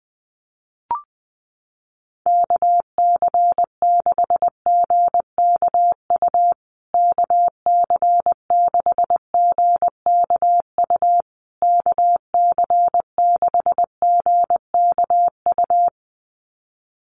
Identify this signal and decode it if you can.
Morse code, heard as a keyed tone, about 20 wpm, 700 Hz. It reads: KC6GKU KC6GKU KC6GKU
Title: USA Call Signs Repeated 3x 20wpm